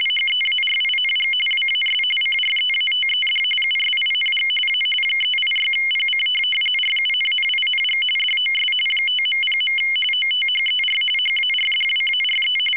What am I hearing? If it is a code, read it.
RTTY, about 45.45 baud. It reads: HEARING WAS CREATED USING A 386PC RUNNING DOS 5.0. OTHER HARDWARE